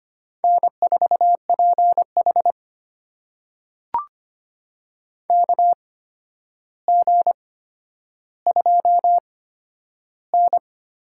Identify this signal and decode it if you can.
Morse code, heard as a keyed tone, about 25 wpm, spaced as for 11 wpm, 700 Hz. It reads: N4PH KG2N
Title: USA Call Signs - 4 Characters - 8x Character Spacing - 25wpm